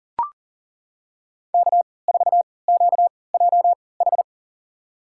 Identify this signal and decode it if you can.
Morse code, heard as a keyed tone, about 38 wpm, spaced as for 22 wpm, 700 Hz. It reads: K4QJH
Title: USA Call Signs - 5 Characters - 3x Character Spacing - 40wpm